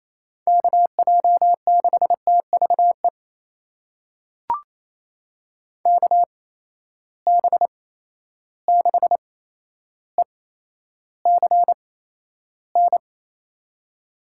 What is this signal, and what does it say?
Morse code, heard as a keyed tone, about 28 wpm, spaced as for 12 wpm, 700 Hz. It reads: KJ6TVE KB6ECN
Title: USA Call Signs - 6 Characters - 8x Character Spacing - 28wpm